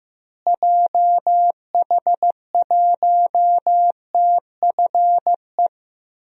Morse code, heard as a keyed tone, about 15 wpm, 700 Hz. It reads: JH1TFE
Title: DXCC-100 Call Signs - 6 Characters - Repeated 2x - 15wpm